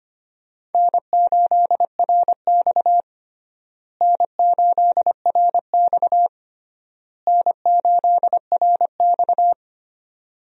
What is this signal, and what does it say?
Morse code, heard as a keyed tone, about 25 wpm, 700 Hz. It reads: N8RX N8RX N8RX
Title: USA Call Signs - 4 Characters - Repeated 3x - 25wpm